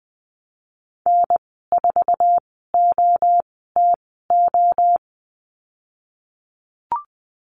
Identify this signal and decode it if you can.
Morse code, heard as a keyed tone, about 20 wpm, spaced as for 14 wpm, 700 Hz. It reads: N4OTO
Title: 167 Lesson 27g - Number 7 - Call Signs 20wpm